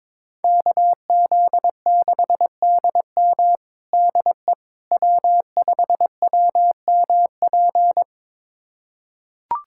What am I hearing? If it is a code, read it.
Morse code, heard as a keyed tone, about 21 wpm, 700 Hz. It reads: KZ6DM DE W5WMP
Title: USA Callsign DE Callsign - 5 Characters - 22wpm